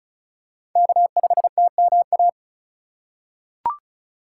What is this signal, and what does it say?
Morse code, heard as a keyed tone, about 36 wpm, 700 Hz. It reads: K5TMA
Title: USA Call Signs - 2x Character Spacing - 35wpm